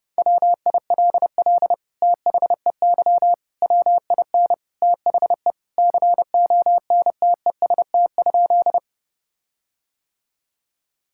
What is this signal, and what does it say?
Morse code, heard as a keyed tone, about 32 wpm, 700 Hz. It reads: WILL THEY WIN THE CONTEST?
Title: Sentences from Top 1600 Words 30wpm